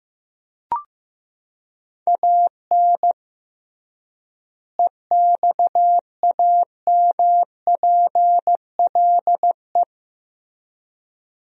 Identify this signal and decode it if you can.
Morse code, heard as a keyed tone, about 15 wpm, 700 Hz. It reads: AN EXAMPLE
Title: 3x Word-Spacing - Sentences from Top 400 Words - 15wpm